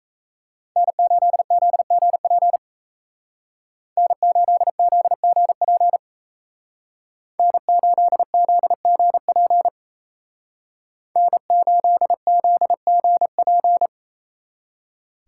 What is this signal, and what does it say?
Morse code, heard as a keyed tone, about 36 wpm, 700 Hz. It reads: N8ZGP N8ZGP N8ZGP N8ZGP